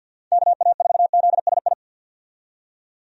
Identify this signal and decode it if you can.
Morse code, heard as a keyed tone, about 52 wpm, 700 Hz. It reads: KA4ZSI
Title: USA Call Signs - 6 Characters - 50wpm